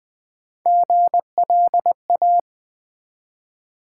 Morse code, heard as a keyed tone, about 20 wpm, 700 Hz. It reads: GLA
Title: Build A Word - 1 Letter At A Time - 5-Letter Words - 20wpm